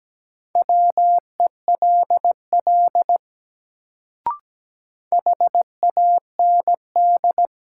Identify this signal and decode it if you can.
Morse code, heard as a keyed tone, about 17 wpm, 700 Hz. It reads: WELL HAND
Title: Sentences from Top 200 Words - New Word Drill - 17wpm